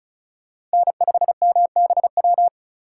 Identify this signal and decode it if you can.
Morse code, heard as a keyed tone, about 34 wpm, 700 Hz. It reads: N5MBW
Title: USA Call Signs - 5 Characters - 4x Character Spacing - 35wpm